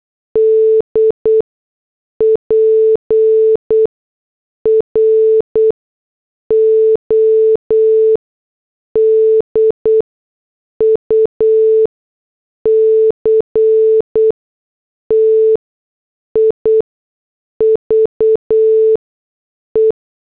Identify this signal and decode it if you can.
Morse code, heard as a keyed tone, about 8 wpm, spaced as for 6 wpm, 440 Hz. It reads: D PRODUCTIVE